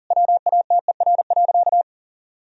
Morse code, heard as a keyed tone, about 40 wpm, 700 Hz. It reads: WATER.
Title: Sentences from Top 1000 Words - Review Part 4 - 40wpm